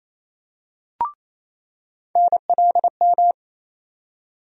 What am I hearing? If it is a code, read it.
Morse code, heard as a keyed tone, about 28 wpm, 700 Hz. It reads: NLM
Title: USA Call Sign Postfixes - 3 Letters 28wpm